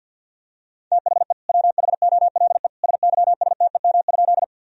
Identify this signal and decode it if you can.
Morse code, heard as a keyed tone, about 51 wpm, 700 Hz. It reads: THE WHOLE SYSTEM?
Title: Sentences from Top 1000 Words - Review Part 3 - 50wpm